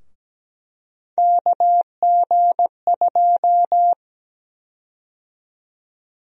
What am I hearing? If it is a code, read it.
Morse code, heard as a keyed tone, about 17 wpm, 700 Hz. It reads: KG2
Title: USA Call Sign Prefixes - Repeated 2x - 17wpm